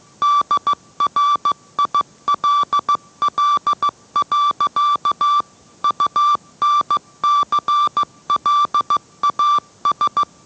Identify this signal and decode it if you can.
Morse code, heard as a keyed tone, about 15 wpm, 1170 Hz. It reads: DRILL. UNCLAS